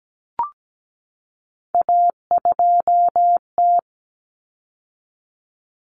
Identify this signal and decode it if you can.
Morse code, heard as a keyed tone, about 17 wpm, 700 Hz. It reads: A2T
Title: Common Cut Numbers - 3-Digit 17wpm